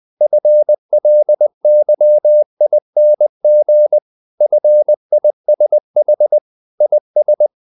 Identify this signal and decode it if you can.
Morse code, heard as a keyed tone, about 20 wpm, 600 Hz. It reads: FLYING FISH IS
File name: the old man and the sea / the old man and the sea part 1 20wpm